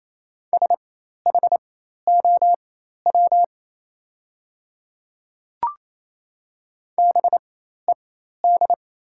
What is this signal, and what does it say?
Morse code, heard as a keyed tone, about 28 wpm, spaced as for 13 wpm, 690 Hz. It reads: SHOW BED
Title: 4x Character Spacing - Top 500 Words - 28wpm